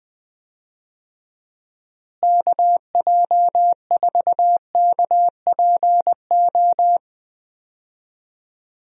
Morse code, heard as a keyed tone, about 20 wpm, 700 Hz. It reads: KJ4KPO
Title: USA Call Signs - 6 Characters - Repeated 2x - 20wpm